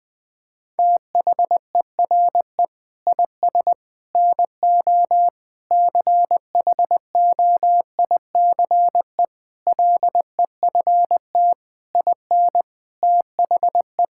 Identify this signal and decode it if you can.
Morse code, heard as a keyed tone, about 20 wpm, 700 Hz. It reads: THERE IS NO CHOICE LEFT IN THE
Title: Sentences from Top 1000 Words - Review Part 6 - 20wpm